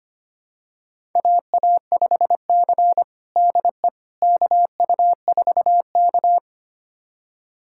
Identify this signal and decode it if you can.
Morse code, heard as a keyed tone, about 25 wpm, 700 Hz. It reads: AA5C DE KU4K
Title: USA Callsign DE Callsign - 4 Characters - 25wpm